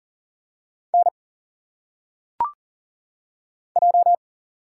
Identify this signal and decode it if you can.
Morse code, heard as a keyed tone, about 40 wpm, 700 Hz. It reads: N J
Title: Single Letters - Phonetic Pronunciation 40wpm